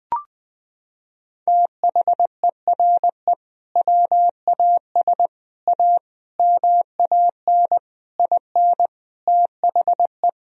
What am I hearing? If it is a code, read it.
Morse code, heard as a keyed tone, about 20 wpm, 700 Hz. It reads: THERE WAS A MAN IN THE
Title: Sentences from Top 1000 Words - Review Part 3 - 20wpm